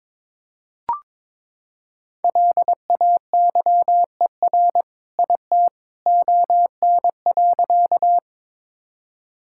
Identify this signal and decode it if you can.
Morse code, heard as a keyed tone, about 22 wpm, 700 Hz. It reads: LAYER IT ON.
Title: Sentences from Top 1600 Words 22wpm